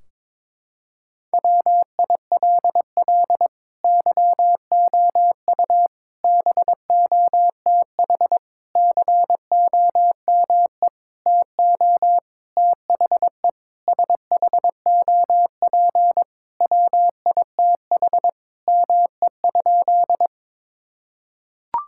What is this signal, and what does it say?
Morse code, heard as a keyed tone, about 22 wpm, 700 Hz. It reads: WILL YOU BOTH COME TO THE SHOP WITH ME?
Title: Sentences from Top 1600 Words 22wpm